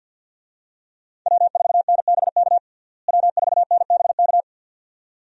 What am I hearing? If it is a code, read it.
Morse code, heard as a keyed tone, about 52 wpm, 700 Hz. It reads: W4NBK W4NBK
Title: USA Call Signs - 5 Characters - Repeated 2x - 50wpm